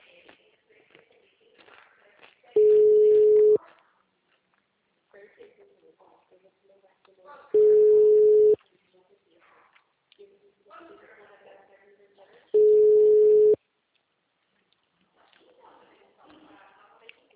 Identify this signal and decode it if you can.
Ringback tone (Europe, 425 Hz), with speech.